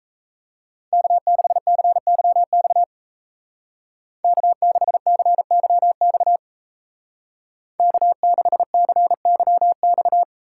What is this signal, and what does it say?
Morse code, heard as a keyed tone, about 35 wpm, 700 Hz. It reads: K6CYX K6CYX K6CYX